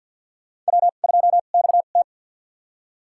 Morse code, heard as a keyed tone, about 50 wpm, spaced as for 36 wpm, 700 Hz. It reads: W2XT
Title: USA Call Signs - 2x Character Spacing - 50wpm